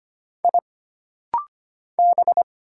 Morse code, heard as a keyed tone, about 25 wpm, 700 Hz. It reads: I B
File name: Single Character (Letter:Number:Punctuation:Prosign) - Rapid-Fire - 25wpm